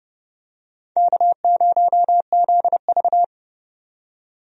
Morse code, heard as a keyed tone, about 32 wpm, 700 Hz. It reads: K0ZV
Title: USA Call Signs - 4 Characters - 30wpm